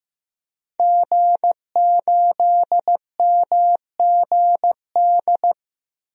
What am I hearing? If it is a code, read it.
Morse code, heard as a keyed tone, about 15 wpm, 700 Hz. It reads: G8MGD
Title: DXCC-100 Call Signs - 5 Characters - Repeated 2x - 15wpm